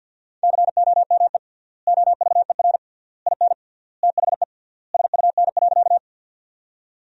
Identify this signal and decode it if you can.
Morse code, heard as a keyed tone, about 47 wpm, 700 Hz. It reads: COME OVER IN THE SUN.
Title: 3x Word-Spacing - Sentences from Top 300 Words - 50wpm